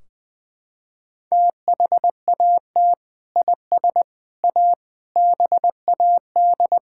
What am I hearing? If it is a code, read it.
Morse code, heard as a keyed tone, about 20 wpm, 700 Hz. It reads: THAT IS A BAD
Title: Sentences from Top 1600 Words 20wpm